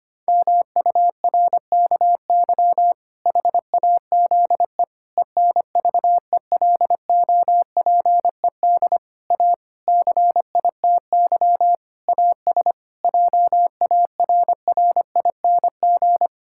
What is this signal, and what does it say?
Morse code, heard as a keyed tone, about 25 wpm, 700 Hz. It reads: MURKY HAZE ENVELOPED A CITY AS JARRING
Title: Pangram Challenge v2 - 25wpm